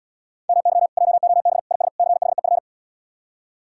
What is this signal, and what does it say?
Morse code, heard as a keyed tone, about 75 wpm, 700 Hz. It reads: MY JOB IS CHEF